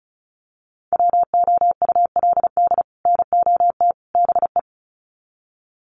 Morse code, heard as a keyed tone, about 34 wpm, 700 Hz. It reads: WOULD NOT BE
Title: Sets of 3 Words 35wpm